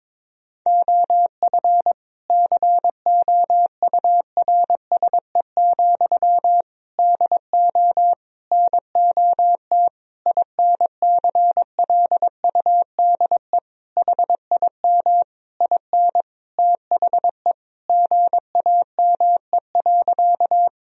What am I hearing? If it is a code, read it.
Morse code, heard as a keyed tone, about 22 wpm, 700 Hz. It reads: OF COURSE, DO NOT INCLUDE HIM IN THE GAME.
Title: Sentences from Top 1000 Words - Review Part 3 - 22wpm